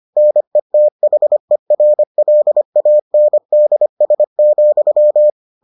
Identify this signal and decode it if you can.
Morse code, heard as a keyed tone, about 25 wpm, 600 Hz. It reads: NETHERLANDS,